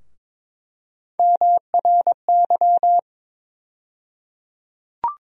Morse code, heard as a keyed tone, about 22 wpm, 700 Hz. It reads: MRY
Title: USA Call Sign Postfixes - 3 Letters 22wpm